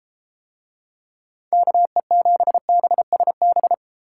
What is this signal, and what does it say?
Morse code, heard as a keyed tone, about 32 wpm, 700 Hz. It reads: KE7BSB